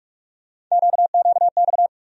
Morse code, heard as a keyed tone, about 46 wpm, 700 Hz. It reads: QQX
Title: USA Call Sign Postfixes - 3 Letters 45wpm